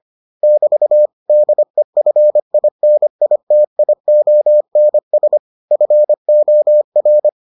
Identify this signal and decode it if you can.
Morse code, heard as a keyed tone, about 25 wpm, 600 Hz. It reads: <BT> DEFINITIONS FOR